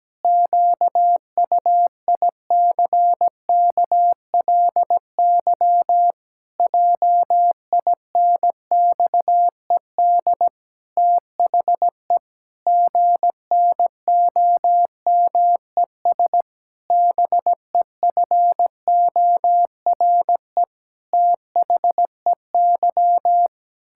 Morse code, heard as a keyed tone, about 17 wpm, 700 Hz. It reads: QUICKLY JINXED THE GNOMES BEFORE THEY
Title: Pangram Challenge v2 - 17wpm